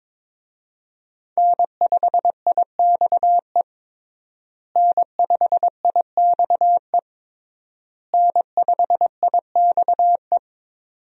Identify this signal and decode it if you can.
Morse code, heard as a keyed tone, about 22 wpm, 700 Hz. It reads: N5IXE N5IXE N5IXE
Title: USA Call Signs - 5 Characters - Repeated 3x - 22wpm